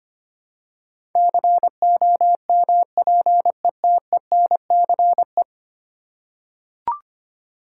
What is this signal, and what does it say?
Morse code, heard as a keyed tone, about 25 wpm, 700 Hz. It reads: COMPETENCE
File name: Ten Letter Words - 25wpm